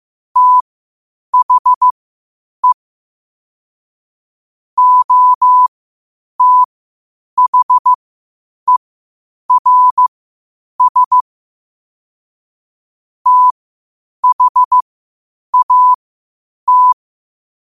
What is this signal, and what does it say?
Morse code, heard as a keyed tone, about 15 wpm, spaced as for 8 wpm, 1000 Hz. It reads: THE OTHERS THAT
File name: qotdfiles / 8wpm